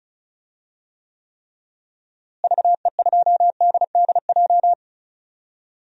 Morse code, heard as a keyed tone, about 36 wpm, 700 Hz. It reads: VE2DDJ
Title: DXCC-100 Call Signs - 6 Characters - Repeated 3x - 35wpm